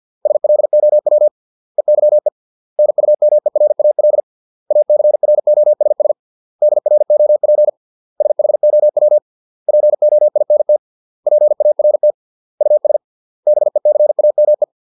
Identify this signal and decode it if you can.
Morse code, heard as a keyed tone, about 50 wpm, 600 Hz. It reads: SLOW EYE NUMERAL ACROSS DROP SHOW POINT PART US BECAME